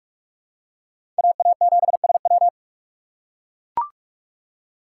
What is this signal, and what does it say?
Morse code, heard as a keyed tone, about 44 wpm, 700 Hz. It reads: AA7SW
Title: USA Call Signs - 5 Characters - 3x Character Spacing - 45wpm